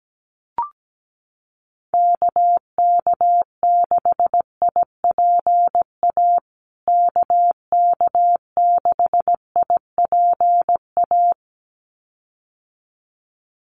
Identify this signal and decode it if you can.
Morse code, heard as a keyed tone, about 17 wpm, 700 Hz. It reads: KK6IPA KK6IPA
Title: USA Call Signs Repeated 2x 17wpm